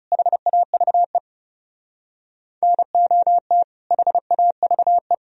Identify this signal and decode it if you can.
Morse code, heard as a keyed tone, about 36 wpm, 700 Hz. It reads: HAVE NOT HAVE
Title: Speed-Racing - Sets of 2 Words 30wpm